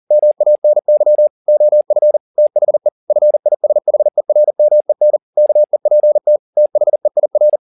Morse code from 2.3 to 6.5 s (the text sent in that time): THE FISHERMEN KEPT